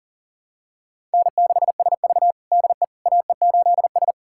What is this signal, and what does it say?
Morse code, heard as a keyed tone, about 40 wpm, 700 Hz. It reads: N6SV DE AE8S
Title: USA Callsign DE Callsign - 4 Characters - 40wpm